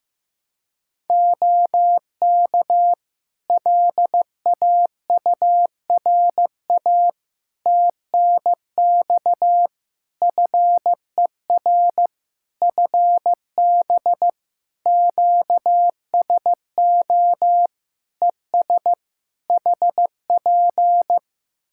Morse code, heard as a keyed tone, about 15 wpm, 700 Hz. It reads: OK LAURA TNX FER FB QSO ES HP
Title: K1USN SST QSO Practice 15wpm